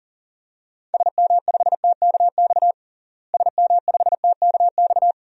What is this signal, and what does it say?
Morse code, heard as a keyed tone, about 40 wpm, 700 Hz. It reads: SM5TKX SM5TKX
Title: DXCC-100 Call Signs - 6 Characters - Repeated 2x - 40wpm